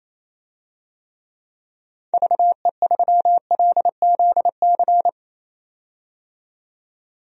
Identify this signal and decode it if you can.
Morse code, heard as a keyed tone, about 28 wpm, 700 Hz. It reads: VE3LZC
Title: DXCC-100 Call Signs - 6 Characters - 28wpm